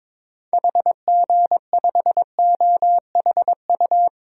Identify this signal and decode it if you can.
Morse code, heard as a keyed tone, about 22 wpm, 700 Hz. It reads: HG5OHU
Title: DXCC-100 Call Signs - 6 Characters - Repeated 3x - 22wpm